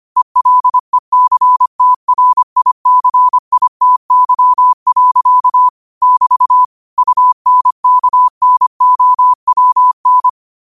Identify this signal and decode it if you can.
Morse code, heard as a keyed tone, about 25 wpm, 1000 Hz. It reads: ELECTRICITY. <BT> UNKNOWN
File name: qotdfiles / 25wpm